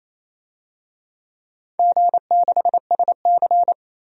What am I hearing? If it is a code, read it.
Morse code, heard as a keyed tone, about 28 wpm, 700 Hz. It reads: G6SC
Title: DXCC-100 Call Signs - 4 Characters - Repeated 3x - 28wpm